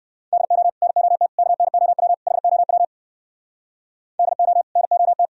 Morse code, heard as a keyed tone, about 59 wpm, 700 Hz. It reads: DO NOT KNOW HOW DO NOT